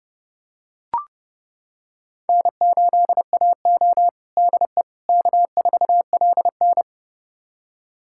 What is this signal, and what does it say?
Morse code, heard as a keyed tone, about 32 wpm, 700 Hz. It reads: N8AO DE K4LN
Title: USA Callsign DE Callsign - 4 Characters - 30wpm